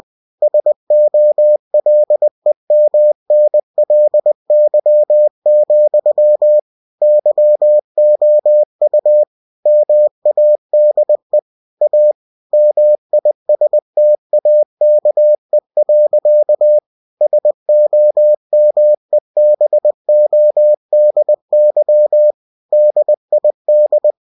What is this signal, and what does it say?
Morse code, heard as a keyed tone, about 20 wpm, 600 Hz. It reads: SOLEMNLY, YOU MADE A MISTAKE. SOMEBODY DID
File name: cw-books / winnie-the-pooh / winnie-20wpm-page-0032